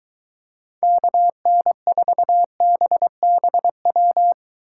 Morse code, heard as a keyed tone, about 23 wpm, 700 Hz. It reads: KN4BBW